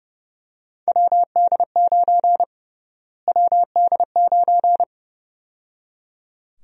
Morse code, heard as a keyed tone, about 29 wpm, 700 Hz. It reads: WD9 WD9
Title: USA Call Sign Prefixes - Repeated 2x - 30wpm